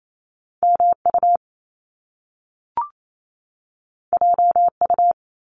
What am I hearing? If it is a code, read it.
Morse code, heard as a keyed tone, about 28 wpm, 700 Hz. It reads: MU JU
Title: Top Two Letter Combinations in Words 28wpm